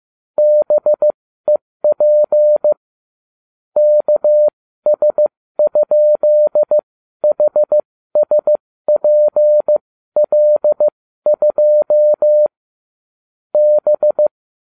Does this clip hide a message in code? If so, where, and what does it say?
Yes: Morse code from 0.4 to 14.3 s, heard as a keyed tone, about 15 wpm, spaced as for 13 wpm, 600 Hz. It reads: BEP KS?HSPL2 B